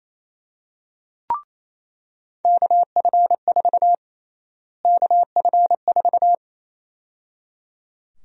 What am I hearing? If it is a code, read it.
Morse code, heard as a keyed tone, about 28 wpm, 700 Hz. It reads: KF4 KF4
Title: USA Call Sign Prefixes - Repeated 2x - 28wpm